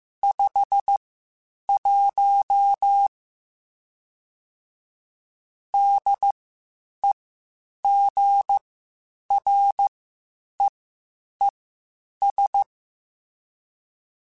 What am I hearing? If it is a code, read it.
Morse code, heard as a keyed tone, about 15 wpm, spaced as for 8 wpm, 800 Hz. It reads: 51 DEGREES
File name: Spaced Characters / Spaced Characters-mp3s / Weather 4 21 sp @13